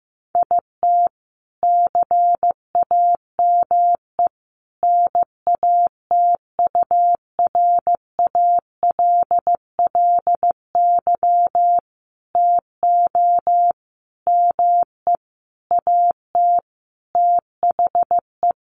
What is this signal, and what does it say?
Morse code, heard as a keyed tone, about 15 wpm, 700 Hz. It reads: IT CAME NATURALLY TO ME AT THE
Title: Sentences from Top 1300 Words 15wpm